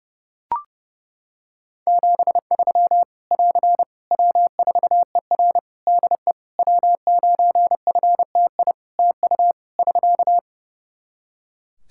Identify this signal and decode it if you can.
Morse code, heard as a keyed tone, about 30 wpm, 700 Hz. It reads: 73 <AR> W4ER DE W9FTI TU <SK>
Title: Practice QSO - Exchange 4 - Ending - 30wpm